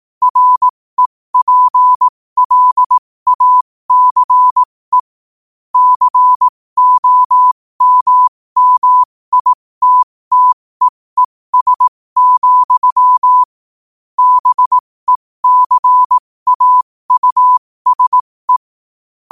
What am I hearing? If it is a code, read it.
Morse code, heard as a keyed tone, about 18 wpm, spaced as for 15 wpm, 1000 Hz. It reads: REPLACE COMMITTEES, BECAUSE
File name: qotdfiles / 15wpm